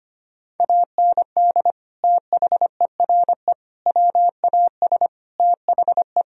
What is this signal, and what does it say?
Morse code, heard as a keyed tone, about 25 wpm, 700 Hz. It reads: AND THERE WAS THE
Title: Sentences from Top 1000 Words - Review Part 1 - 25wpm